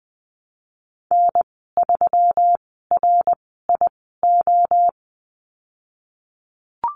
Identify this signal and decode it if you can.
Morse code, heard as a keyed tone, about 20 wpm, spaced as for 14 wpm, 700 Hz. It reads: N3RIO
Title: 122 Lesson 20g - Number 3 - Call Signs 20wpm